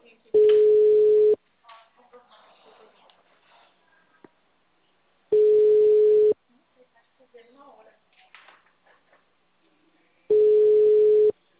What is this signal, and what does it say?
Ringback tone (Europe, 425 Hz), with speech.